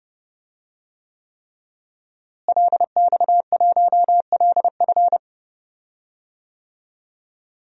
Morse code, heard as a keyed tone, about 30 wpm, 700 Hz. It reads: LX1LF
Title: DXCC-100 Call Signs - 5 Characters - Repeated 3x - 30wpm